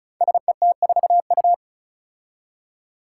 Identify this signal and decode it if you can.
Morse code, heard as a keyed tone, about 36 wpm, 700 Hz. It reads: SET4U
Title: Five Challenging Letter-Number Combinations 35wpm